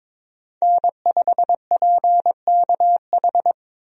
Morse code, heard as a keyed tone, about 22 wpm, 700 Hz. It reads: N5PKH
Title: USA Call Signs - 5 Characters - Repeated 2x - 22wpm